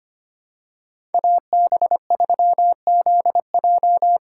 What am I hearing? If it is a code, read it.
Morse code, heard as a keyed tone, about 25 wpm, 700 Hz. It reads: AB3ZJ
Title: USA Call Signs - 5 Characters - 2x Character Spacing - 25wpm